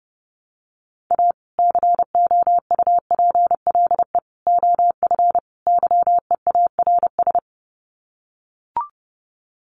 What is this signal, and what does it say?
Morse code, heard as a keyed tone, about 30 wpm, 700 Hz. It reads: A COUPLE OF YEARS
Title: Sets of 4 Words 30wpm